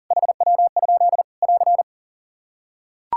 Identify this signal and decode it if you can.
Morse code, heard as a keyed tone, about 40 wpm, 700 Hz. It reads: HW? <AR>
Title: K1USN SST QSO Practice 40wpm